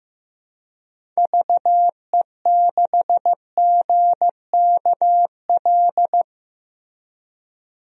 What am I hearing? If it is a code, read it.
Morse code, heard as a keyed tone, about 15 wpm, 700 Hz. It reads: VE6GKL
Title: DXCC-100 Call Signs - 6 Characters - Repeated 2x - 15wpm